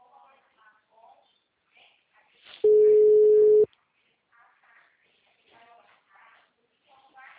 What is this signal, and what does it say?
Ringback tone (Europe, 425 Hz), with speech.
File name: out- -2023- - - .49109